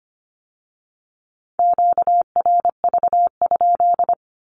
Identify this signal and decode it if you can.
Morse code, heard as a keyed tone, about 25 wpm, 700 Hz. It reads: QRV?
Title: Q-Codes with Spoken Meaning - 25wpm